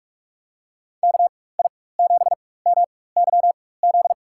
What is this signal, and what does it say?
Morse code, heard as a keyed tone, about 42 wpm, spaced as for 20 wpm, 700 Hz. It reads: KI7MYZ
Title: USA Call Signs - 6 Characters - 4x Character Spacing - 45wpm